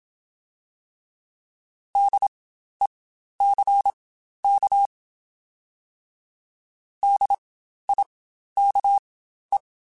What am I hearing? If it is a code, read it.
Morse code, heard as a keyed tone, about 27 wpm, spaced as for 12 wpm, 800 Hz. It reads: DECK DIKE
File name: Spaced Alphas / Spaced Alphas mp3s / Ds sp @25